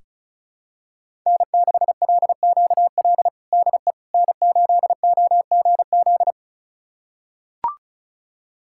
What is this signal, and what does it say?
Morse code, heard as a keyed tone, about 36 wpm, 700 Hz. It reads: N6LQL DE N8OGZ
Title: USA Callsign DE Callsign - 5 Characters - 35wpm